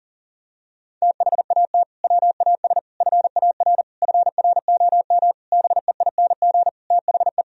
Morse code, heard as a keyed tone, about 40 wpm, 700 Hz. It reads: THAT WAS FAR FROM BEING THE
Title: Sentences from Top 1000 Words - Review Part 3 - 40wpm